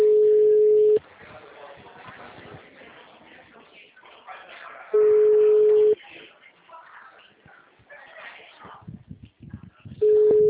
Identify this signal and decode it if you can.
Ringback tone (Europe, 425 Hz), with speech.
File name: out- -2055- - - .23